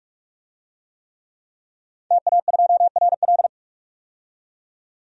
Morse code, heard as a keyed tone, about 43 wpm, 700 Hz. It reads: TA2RL
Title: DXCC-100 Call Signs - 5 Characters - 45wpm